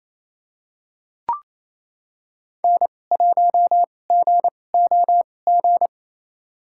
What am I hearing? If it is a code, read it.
Morse code, heard as a keyed tone, about 28 wpm, spaced as for 20 wpm, 700 Hz. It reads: N1GOG
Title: USA Call Signs - 2x Character Spacing - 28wpm